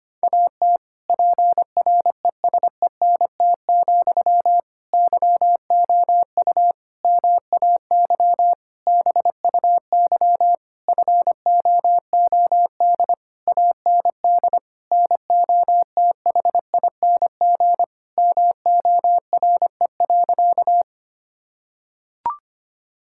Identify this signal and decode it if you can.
Morse code, heard as a keyed tone, about 25 wpm, 700 Hz. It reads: AT PRESENT, YOU MAY BUY FOOD AND NOTHING MORE.
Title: Sentences from Top 1000 Words - Review Part 5 - 25wpm